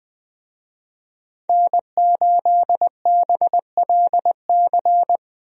Morse code, heard as a keyed tone, about 21 wpm, 700 Hz. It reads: N8BLC
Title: USA Call Signs - 5 Characters - Repeated 2x - 20wpm